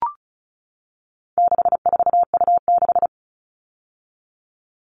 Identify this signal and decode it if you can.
Morse code, heard as a keyed tone, about 34 wpm, 700 Hz. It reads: 64U6
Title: All Cut Numbers - 4-Digit 35wpm